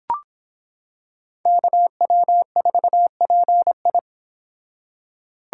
Morse code, heard as a keyed tone, about 25 wpm, 700 Hz. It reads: KW4PI